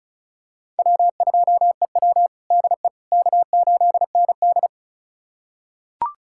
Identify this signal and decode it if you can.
Morse code, heard as a keyed tone, about 36 wpm, 700 Hz. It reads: W2EW DE K8ND
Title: USA Callsign DE Callsign - 4 Characters - 35wpm